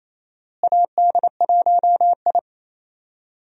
Morse code, heard as a keyed tone, about 28 wpm, 700 Hz. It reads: AD1I
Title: USA Call Signs - 4 Characters - Repeated 2x - 28wpm